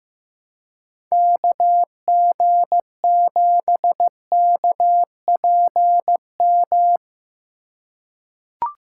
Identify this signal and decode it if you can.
Morse code, heard as a keyed tone, about 15 wpm, 700 Hz. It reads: KG7KPM
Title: USA Call Signs - 6 Characters - 3x Character Spacing - 15wpm